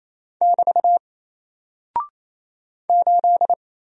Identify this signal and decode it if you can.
Morse code, heard as a keyed tone, about 27 wpm, 700 Hz. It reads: <BT> 8
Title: Single Character (Letter:Number:Punctuation:Prosign) Alt 28wpm